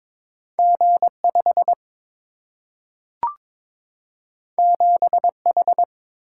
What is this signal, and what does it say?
Morse code, heard as a keyed tone, about 22 wpm, 700 Hz. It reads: G 5 7 H
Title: Single-Letter Single-Number 22wpm